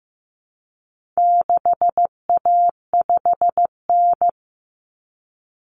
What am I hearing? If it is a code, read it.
Morse code, heard as a keyed tone, about 15 wpm, 700 Hz. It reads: 6A5N
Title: Common Cut Numbers - 4-Digit 15wpm